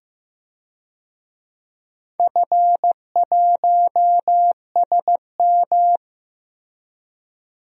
Morse code, heard as a keyed tone, about 15 wpm, 700 Hz. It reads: F1SM
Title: DXCC-100 Call Signs - 4 Characters - Repeated 3x - 15wpm